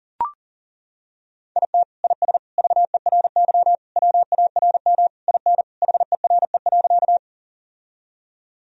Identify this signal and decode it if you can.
Morse code, heard as a keyed tone, about 40 wpm, 700 Hz. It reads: IT IS VERY WARM IN HERE.
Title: Sentences from Top 1000 Words - Review Part 2 - 40wpm